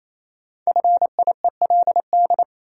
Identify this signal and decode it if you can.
Morse code, heard as a keyed tone, about 28 wpm, 700 Hz. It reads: FIELD
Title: Sentences from Top 1000 Words - New Word Drill - 28wpm